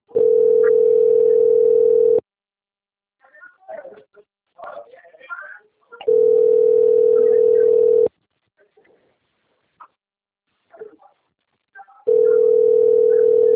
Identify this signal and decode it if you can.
Ringback tone (North America), 440 + 480 Hz, with speech.
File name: out- - - - - .16435